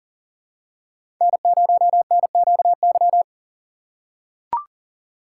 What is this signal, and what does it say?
Morse code, heard as a keyed tone, about 40 wpm, 700 Hz. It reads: N0NQY
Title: USA Call Signs - 5 Characters - 4x Character Spacing - 40wpm